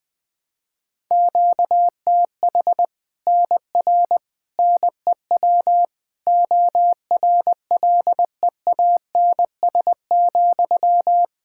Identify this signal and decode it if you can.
Morse code, heard as a keyed tone, about 20 wpm, 700 Hz. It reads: QTH NR NEW ORLEANS,
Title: Practice QSO - Exchange 1 - Beginning - 20wpm